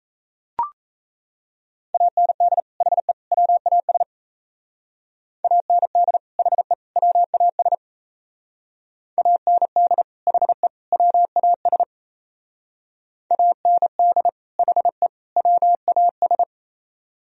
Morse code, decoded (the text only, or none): AND HE WAS AND HE WAS AND HE WAS AND HE WAS